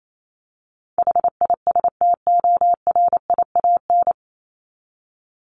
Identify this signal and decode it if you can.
Morse code, heard as a keyed tone, about 28 wpm, 700 Hz. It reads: HISTORIAN
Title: Top 1751-2000 Words 28wpm